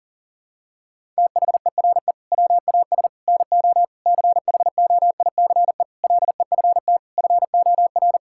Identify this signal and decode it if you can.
Morse code, heard as a keyed tone, about 40 wpm, 700 Hz. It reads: THERE WAS NO CHOICE LEFT FOR
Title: Sentences from Top 1000 Words - Review Part 6 - 40wpm